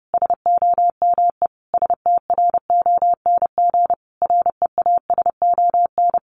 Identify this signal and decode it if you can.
Morse code, heard as a keyed tone, about 30 wpm, 700 Hz. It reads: SOME STRONG REASON
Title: Sentences from Top 700 Words 30wpm